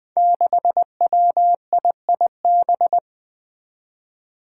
Morse code, heard as a keyed tone, about 20 wpm, 700 Hz. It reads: 6WIIB
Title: Five Challenging Letter-Number Combinations 20wpm